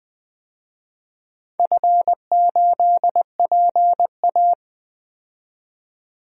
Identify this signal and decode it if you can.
Morse code, heard as a keyed tone, about 20 wpm, 700 Hz. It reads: F8PA
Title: DXCC-100 Call Signs - 4 Characters - 20wpm